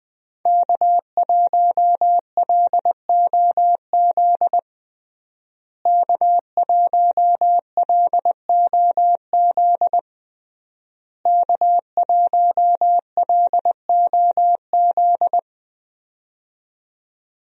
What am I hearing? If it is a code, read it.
Morse code, heard as a keyed tone, about 20 wpm, 700 Hz. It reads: K1LOZ K1LOZ K1LOZ
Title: USA Call Signs - 5 Characters - Repeated 3x - 20wpm